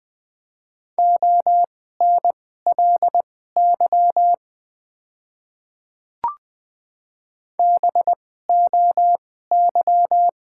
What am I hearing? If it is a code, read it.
Morse code, heard as a keyed tone, about 20 wpm, spaced as for 14 wpm, 700 Hz. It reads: ONLY BOY
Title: 2x Character Spacing - Top 200 Words - 20wpm